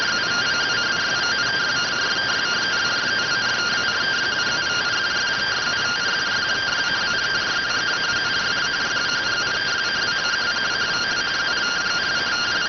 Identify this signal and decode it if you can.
Two-tone teleprinter signal, about 170 Hz shift.